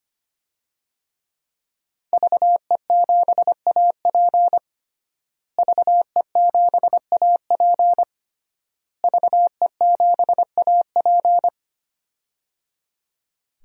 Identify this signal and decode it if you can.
Morse code, heard as a keyed tone, about 25 wpm, 700 Hz. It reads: VE7AP VE7AP VE7AP
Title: DXCC-100 Call Signs - 5 Characters - Repeated 3x - 25wpm